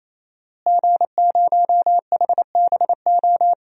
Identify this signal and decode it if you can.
Morse code, heard as a keyed tone, about 27 wpm, 700 Hz. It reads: G0HBO
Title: DXCC-100 Call Signs - 5 Characters - Repeated 3x - 28wpm